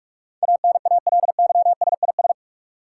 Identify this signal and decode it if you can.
Morse code, heard as a keyed tone, about 45 wpm, 700 Hz. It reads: ANALYSIS
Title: Sentences from Top 1400 Words - New Word Drill Alt - 45wpm